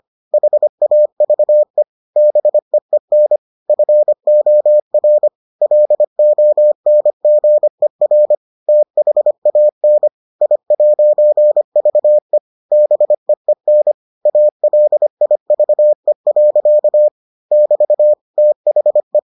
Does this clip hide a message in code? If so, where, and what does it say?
Yes: Morse code from 0.3 to 19.2 s, heard as a keyed tone, about 25 wpm, 600 Hz. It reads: HAVE BEEN FOR LONGER THAN I'VE BEEN ALIVE. <BT> THE